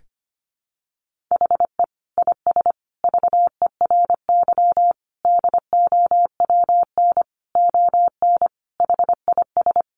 Morse code, heard as a keyed tone, about 25 wpm, 700 Hz. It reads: HE IS VERY DOWN ON HIS
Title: Sentences from Top 1500 Words 25wpm